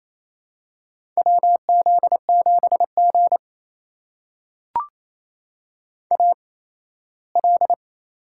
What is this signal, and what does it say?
Morse code, heard as a keyed tone, about 28 wpm, spaced as for 17 wpm, 700 Hz. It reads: WZ7G AL
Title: USA Call Signs - 4 Characters - 8x Character Spacing - 28wpm